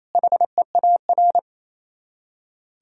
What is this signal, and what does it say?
Morse code, heard as a keyed tone, about 36 wpm, 700 Hz. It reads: HEAR